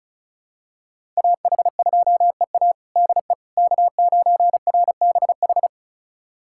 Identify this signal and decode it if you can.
Morse code, heard as a keyed tone, about 34 wpm, 700 Hz. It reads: AH2EA DE K9RBH
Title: USA Callsign DE Callsign - 5 Characters - 35wpm